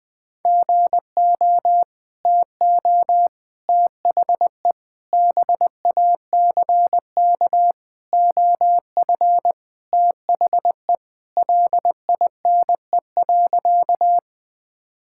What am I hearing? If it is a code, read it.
Morse code, heard as a keyed tone, about 20 wpm, 700 Hz. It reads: GO TO THE BACK OF THE LINE.
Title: Sentences with 5-Word Phrases - 20wpm